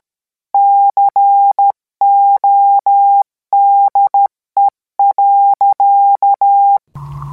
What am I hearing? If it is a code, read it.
Morse code, heard as a keyed tone, about 13 wpm, 800 Hz. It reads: CODE.